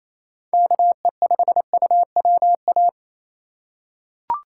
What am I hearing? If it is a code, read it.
Morse code, heard as a keyed tone, about 28 wpm, 700 Hz. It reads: KE5UWA
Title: USA Call Signs - 6 Characters - 2x Character Spacing - 28wpm